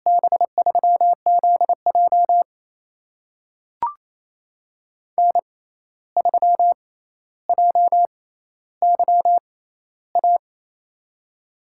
Morse code, heard as a keyed tone, about 28 wpm, spaced as for 13 wpm, 700 Hz. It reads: B3ZJ N3JYA
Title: USA Call Signs - 5 Characters - 6x Character Spacing - 28wpm